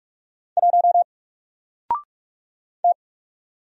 Morse code, heard as a keyed tone, about 46 wpm, 700 Hz. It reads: 1 T
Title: Single Letter-Number 45wpm